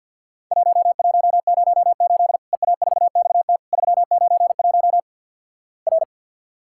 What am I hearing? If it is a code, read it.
Morse code, heard as a keyed tone, about 47 wpm, 700 Hz. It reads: 1108 EA4XT 391 R
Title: Worked All Europe DX Contest - QTC Exchange - 50wpm